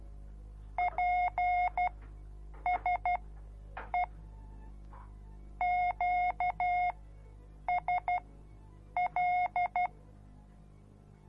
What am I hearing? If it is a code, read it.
Morse code, heard as a keyed tone, about 12 wpm, spaced as for 7 wpm, 720 Hz. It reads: PSE QSL